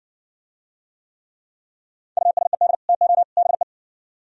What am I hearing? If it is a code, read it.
Morse code, heard as a keyed tone, about 68 wpm, 700 Hz. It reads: USED TO BE